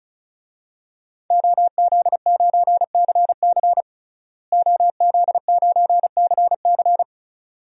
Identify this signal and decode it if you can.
Morse code, heard as a keyed tone, about 36 wpm, 700 Hz. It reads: OZ9CC OZ9CC
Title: DXCC-100 Call Signs - 5 Characters - Repeated 2x - 35wpm